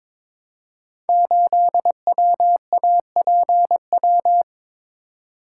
Morse code, heard as a keyed tone, about 22 wpm, 700 Hz. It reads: 8WAPW
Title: Code Groups - 5 Letter:Numbers - 22wpm